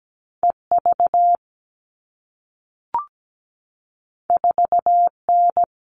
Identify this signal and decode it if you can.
Morse code, heard as a keyed tone, about 17 wpm, 700 Hz. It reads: EV 4N
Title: All Cut Numbers - 2-Digit 17wpm